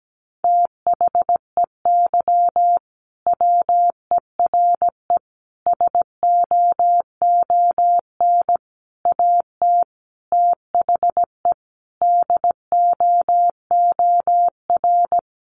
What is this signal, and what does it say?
Morse code, heard as a keyed tone, about 17 wpm, 700 Hz. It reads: THEY WERE SOON AT THE DOOR
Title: Sentences from Top 1200 Words 17wpm